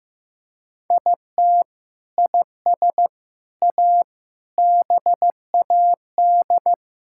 Morse code, heard as a keyed tone, about 15 wpm, 700 Hz. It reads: IT IS A BAD
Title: Sentences from Top 1000 Words - Review Part 5 - 15wpm